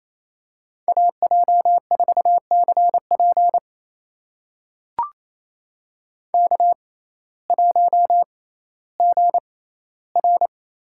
Morse code, heard as a keyed tone, about 28 wpm, spaced as for 15 wpm, 700 Hz. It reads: AJ4CP K1GR
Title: USA Call Signs - 5 Characters - 6x Character Spacing - 28wpm